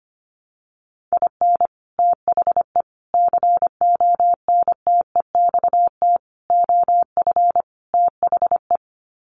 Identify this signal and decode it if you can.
Morse code, heard as a keyed tone, about 25 wpm, 700 Hz. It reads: IN THE CONTEXT OF THE
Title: Sets of 5 Words 25wpm